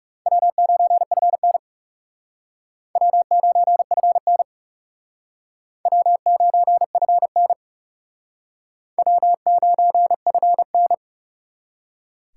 Morse code, heard as a keyed tone, about 38 wpm, 700 Hz. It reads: W9FN W9FN W9FN W9FN